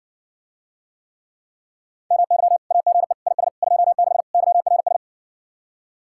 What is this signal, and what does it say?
Morse code, heard as a keyed tone, about 60 wpm, 700 Hz. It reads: MY AGE IS 26 YRS